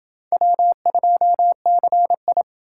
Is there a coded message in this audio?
Yes: Morse code from 0.3 to 2.4 s, heard as a keyed tone, about 27 wpm, 700 Hz. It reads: W2CI